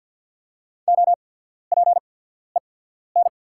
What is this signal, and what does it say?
Morse code, heard as a keyed tone, about 48 wpm, spaced as for 14 wpm, 690 Hz. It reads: OPEN
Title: 8x Character Spacing - Top 500 Words - 50wpm